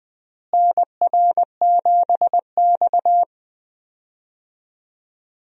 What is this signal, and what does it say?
Morse code, heard as a keyed tone, about 20 wpm, 700 Hz. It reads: NR7X
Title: USA Call Signs - 4 Characters - Repeated 3x - 20wpm